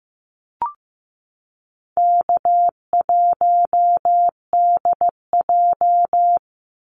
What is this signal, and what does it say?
Morse code, heard as a keyed tone, about 15 wpm, 700 Hz. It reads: K1DJ
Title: Easy Call Signs Repeated 2x - 15wpm